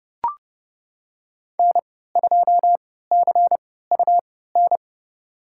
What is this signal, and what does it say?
Morse code, heard as a keyed tone, about 32 wpm, spaced as for 17 wpm, 700 Hz. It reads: N2CUN
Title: USA Call Signs - 5 Characters - 3x Character Spacing - 30wpm